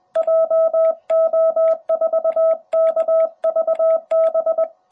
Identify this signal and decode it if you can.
Morse code, heard as a keyed tone, about 20 wpm, 660 Hz. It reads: JO4KVB
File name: jh8cbh / 2021cw